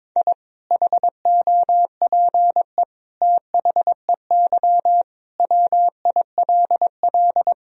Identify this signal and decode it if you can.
Morse code, heard as a keyed tone, about 22 wpm, 700 Hz. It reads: I HOPE THEY WILL
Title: Sentences from Top 1000 Words - Review Part 2 - 22wpm